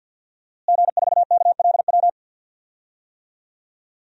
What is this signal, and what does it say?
Morse code, heard as a keyed tone, about 49 wpm, 700 Hz. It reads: G4KLW
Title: DXCC-100 Call Signs - 5 Characters - Repeated 3x - 50wpm